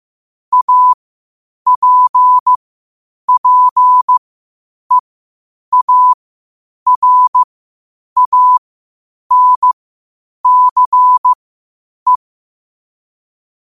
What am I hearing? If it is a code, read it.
Morse code, heard as a keyed tone, about 15 wpm, spaced as for 8 wpm, 1000 Hz. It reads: APPEARANCE
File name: qotdfiles / 8wpm